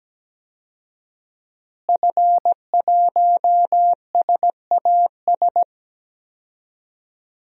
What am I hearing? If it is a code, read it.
Morse code, heard as a keyed tone, about 17 wpm, 700 Hz. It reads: F1SAS
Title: DXCC-100 Call Signs - 5 Characters - Repeated 3x - 17wpm